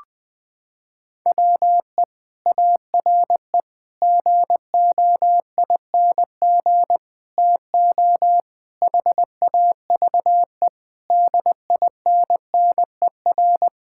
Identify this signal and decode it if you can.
Morse code, heard as a keyed tone, about 20 wpm, 700 Hz. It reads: WE ARE GOING TO HAVE DINNER
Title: Sentences with 3-Word Phrases - 20wpm